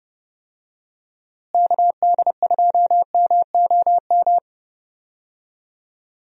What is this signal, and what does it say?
Morse code, heard as a keyed tone, about 30 wpm, 700 Hz. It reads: KD2MOM
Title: USA Call Signs - 6 Characters - Repeated 3x - 30wpm